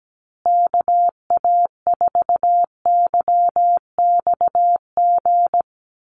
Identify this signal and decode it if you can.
Morse code, heard as a keyed tone, about 17 wpm, 700 Hz. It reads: KA4YXG
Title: USA Call Signs Repeated 2x 17wpm